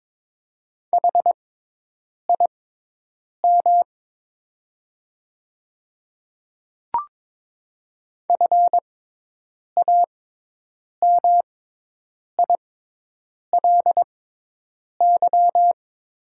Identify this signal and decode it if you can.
Morse code, heard as a keyed tone, about 22 wpm, spaced as for 8 wpm, 700 Hz. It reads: HIM FAMILY
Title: 6x Character Spacing - Top 500 Words - 22wpm